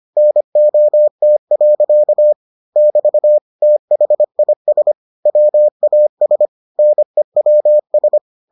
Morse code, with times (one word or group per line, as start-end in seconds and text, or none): 0.2-2.3 NOT.
2.8-3.4 <BT>
3.6-4.9 THIS
5.3-6.5 WAS
6.8-8.2 NEWS